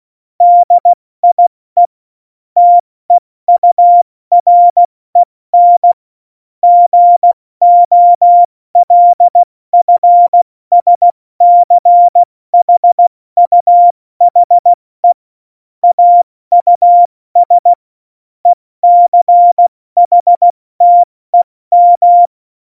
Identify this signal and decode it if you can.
Morse code, heard as a keyed tone, about 16 wpm, 700 Hz. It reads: DIE TEUREN GOLFSCHUHE AUS ECHTEM
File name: Morsekurs / Witze / Witz 24. November 2020, 70 F80